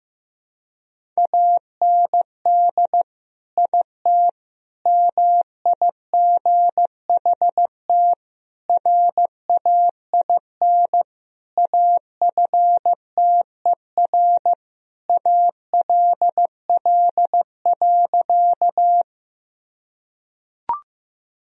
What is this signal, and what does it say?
Morse code, heard as a keyed tone, about 16 wpm, 700 Hz. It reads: AND IT MIGHT RAIN AFTER ALL.
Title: Sentences from Top 1000 Words - Review Part 1 - 15wpm